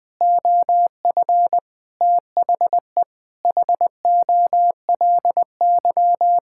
Morse code, heard as a keyed tone, about 20 wpm, 700 Hz. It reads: OF THE HOLY
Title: Sentences with 9-Word Phrases - Spoken Beginning - 20wpm